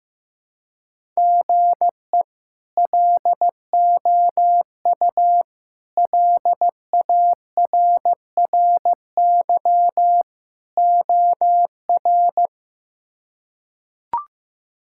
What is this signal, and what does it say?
Morse code, heard as a keyed tone, about 15 wpm, 700 Hz. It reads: GE LOU LARRY OR
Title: K1USN SST QSO Practice - 15wpm